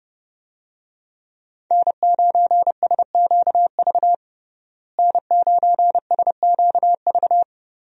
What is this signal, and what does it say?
Morse code, heard as a keyed tone, about 30 wpm, 700 Hz. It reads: N9SQV N9SQV
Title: USA Call Signs - 5 Characters - Repeated 2x - 30wpm